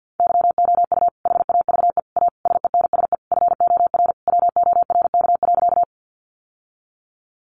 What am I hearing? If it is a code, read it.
Morse code, heard as a keyed tone, about 50 wpm, 700 Hz. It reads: YOU HAVE A SENSE FOR WORK.
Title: Sentences from Top 700 Words 50wpm